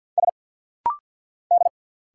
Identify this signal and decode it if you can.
Morse code, heard as a keyed tone, about 50 wpm, 690 Hz. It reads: S D
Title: Single Letter-Number Alt - Rapid-Fire 50wpm